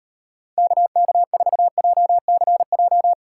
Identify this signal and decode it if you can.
Morse code, heard as a keyed tone, about 38 wpm, 700 Hz. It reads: KK4JCJ